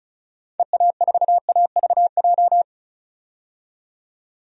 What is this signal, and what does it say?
Morse code, heard as a keyed tone, about 34 wpm, 700 Hz. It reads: EA4AVJ
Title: DXCC-100 Call Signs - 6 Characters - 35wpm